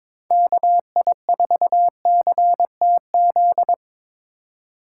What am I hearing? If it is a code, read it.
Morse code, heard as a keyed tone, about 25 wpm, 700 Hz. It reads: KI4CTZ
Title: Speed Racing - USA 2x3 Call Signs 22wpm